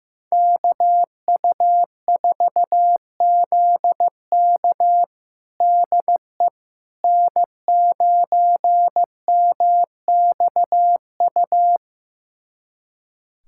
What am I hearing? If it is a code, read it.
Morse code, heard as a keyed tone, about 15 wpm, 700 Hz. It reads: KU4ZK DE N9MXU
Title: USA Callsign DE Callsign - 5 Characters - 15wpm